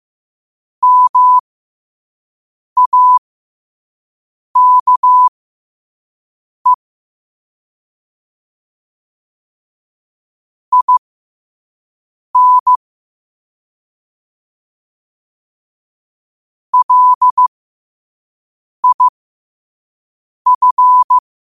Morse code, heard as a keyed tone, about 15 wpm, spaced as for 5 wpm, 1000 Hz. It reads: MAKE IN LIF
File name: 5wpm